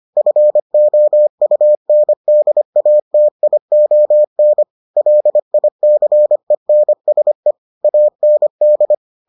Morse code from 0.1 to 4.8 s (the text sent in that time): FOUNDATION